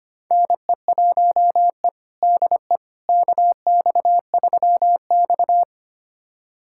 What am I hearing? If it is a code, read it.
Morse code, heard as a keyed tone, about 25 wpm, 700 Hz. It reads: NE1E DE KX3X
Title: USA Callsign DE Callsign - 4 Characters - 25wpm